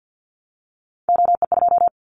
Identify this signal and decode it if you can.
Morse code, heard as a keyed tone, about 50 wpm, 700 Hz. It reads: OE2
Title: Callsign Prefixes for Top 25 DXCC - 50wpm